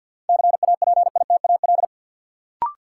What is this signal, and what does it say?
Morse code, heard as a keyed tone, about 53 wpm, 700 Hz. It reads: CAPITAL
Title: Sentences from Top 1000 Words - New Word Drill - 50wpm